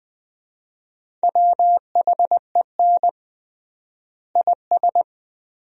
Morse code, heard as a keyed tone, about 20 wpm, 700 Hz. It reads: WHEN IS
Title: 3x Word-Spacing - Sentences from Top 100 Words - 20wpm